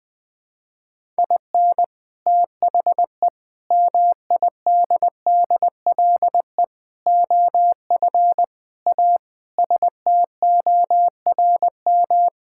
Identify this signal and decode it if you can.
Morse code, heard as a keyed tone, about 20 wpm, 700 Hz. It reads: IN THE MIDDLE OF A STORM
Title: Sentences with 5-Word Phrases - Spoken Beginning - 20wpm